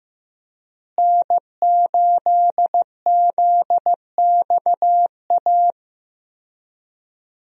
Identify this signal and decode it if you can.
Morse code, heard as a keyed tone, about 15 wpm, 700 Hz. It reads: N8ZXA
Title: USA Call Signs - 5 Characters - Repeated 3x - 15wpm